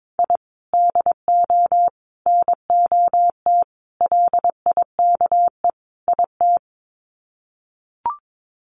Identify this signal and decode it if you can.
Morse code, heard as a keyed tone, about 22 wpm, 700 Hz. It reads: I DO NOT LIKE IT
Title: Sets of 5 Words Encore 22wpm